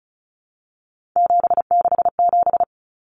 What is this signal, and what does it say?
Morse code, heard as a keyed tone, about 34 wpm, 700 Hz. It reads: 767
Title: Numbers - 3-Digit 35wpm